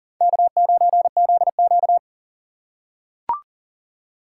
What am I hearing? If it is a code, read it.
Morse code, heard as a keyed tone, about 37 wpm, 700 Hz. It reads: K9ZQ
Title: USA Call Signs - 4 Characters - 4x Character Spacing - 40wpm